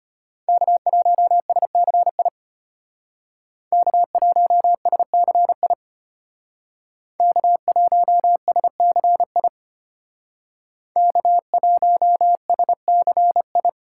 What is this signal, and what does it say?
Morse code, heard as a keyed tone, about 32 wpm, 700 Hz. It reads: K1SCI K1SCI K1SCI K1SCI